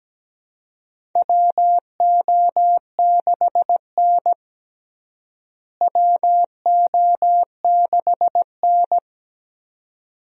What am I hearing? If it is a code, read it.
Morse code, heard as a keyed tone, about 17 wpm, 700 Hz. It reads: WO6N WO6N
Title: USA Call Signs - 4 Characters - Repeated 2x - 17wpm